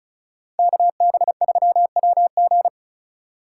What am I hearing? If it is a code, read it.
Morse code, heard as a keyed tone, about 44 wpm, 700 Hz. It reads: KB3WG